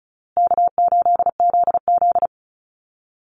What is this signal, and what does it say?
Morse code, heard as a keyed tone, about 34 wpm, 700 Hz. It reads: K8ZZ
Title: Easy Call Signs Repeated 3x - 35wpm